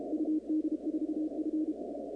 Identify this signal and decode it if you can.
Morse code, heard as a keyed tone, about 33 wpm, 310 Hz. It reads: UD4F
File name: contests / 2019-cw-wpx-kt5j-k5pi- -2330 0-37